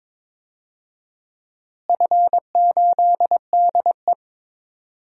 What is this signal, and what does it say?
Morse code, heard as a keyed tone, about 22 wpm, 700 Hz. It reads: F8DE
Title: DXCC-100 Call Signs - 4 Characters - Repeated 3x - 22wpm